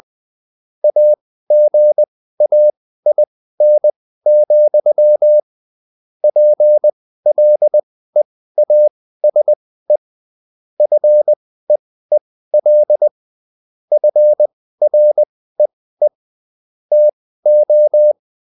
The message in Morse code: AGAIN, PLEASE FEEL FREE TO